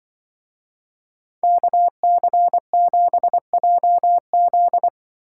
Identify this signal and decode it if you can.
Morse code, heard as a keyed tone, about 24 wpm, 700 Hz. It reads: KC7JZ
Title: Speed Racing - USA 2x2 Call Signs 20wpm